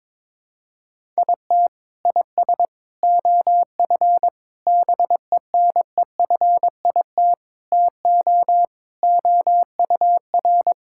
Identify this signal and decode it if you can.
Morse code, heard as a keyed tone, about 21 wpm, 700 Hz. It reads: IT IS OF BENEFIT TO OUR
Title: Sentences from Top 1000 Words - Review Part 6 - 22wpm